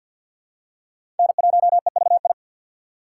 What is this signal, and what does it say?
Morse code, heard as a keyed tone, about 53 wpm, 700 Hz. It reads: N1EVI
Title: USA Call Signs - 5 Characters - 2x Character Spacing - 50wpm